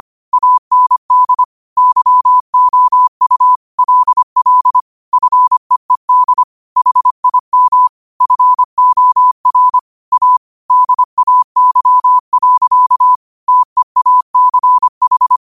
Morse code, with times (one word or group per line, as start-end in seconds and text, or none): 0.3-1.4 AND
1.8-3.6 YOU
3.8-4.8 LL
5.1-6.4 FEED
6.8-7.9 HIM
8.2-9.8 FOR
10.1-10.4 A
10.7-13.2 DAY.
13.5-15.4 TEACH